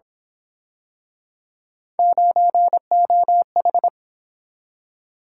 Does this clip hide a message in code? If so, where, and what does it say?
Yes: Morse code from 2.0 to 3.9 s, heard as a keyed tone, about 29 wpm, 700 Hz. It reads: 9OH